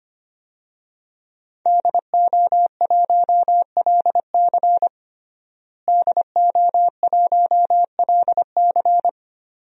Morse code, heard as a keyed tone, about 25 wpm, 700 Hz. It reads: DO1LC DO1LC
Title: DXCC-100 Call Signs - 5 Characters - Repeated 2x - 25wpm